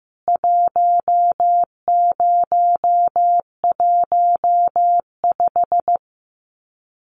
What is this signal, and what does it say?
Morse code, heard as a keyed tone, about 15 wpm, 700 Hz. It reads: 1015
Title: CWT Exchange 15wpm